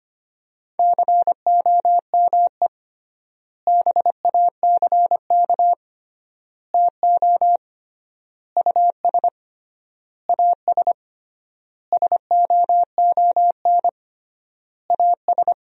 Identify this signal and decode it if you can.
Morse code, heard as a keyed tone, about 25 wpm, 700 Hz. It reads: COME BACK TO US AS SOON AS
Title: 3x Word-Spacing - Sentences from Top 500 Words - 25wpm